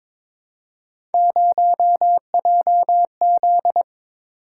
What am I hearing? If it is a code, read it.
Morse code, heard as a keyed tone, about 22 wpm, 700 Hz. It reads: 0JZ
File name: practice / 22wpm / Three Letter-Number Combinations - 22wpm